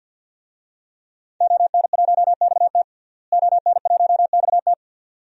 Morse code, heard as a keyed tone, about 47 wpm, 700 Hz. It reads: ON1XT ON1XT
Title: DXCC-100 Call Signs - 5 Characters - Repeated 2x - 50wpm